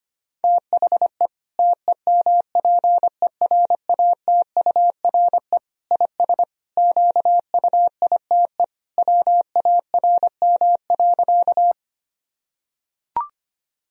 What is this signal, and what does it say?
Morse code, heard as a keyed tone, about 25 wpm, 700 Hz. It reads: THE TEMPERATURE IS QUITE WARM.
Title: Sentences from Top 1000 Words - Review Part 4 - 25wpm